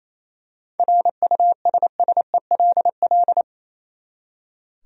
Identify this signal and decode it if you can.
Morse code, heard as a keyed tone, about 28 wpm, 700 Hz. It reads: RUSSELL
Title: USA Common Names - 28wpm